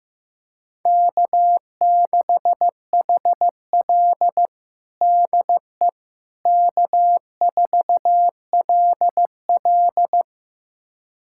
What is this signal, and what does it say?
Morse code, heard as a keyed tone, about 16 wpm, 700 Hz. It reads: K6HL DE K4LL
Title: USA Callsign DE Callsign - 4 Characters - 15wpm